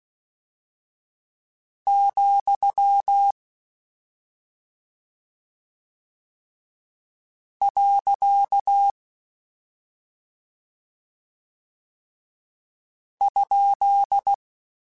Morse code, heard as a keyed tone, about 16 wpm, 800 Hz. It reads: , . ?
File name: Short Runs / Short Runs-mp3s / Words @5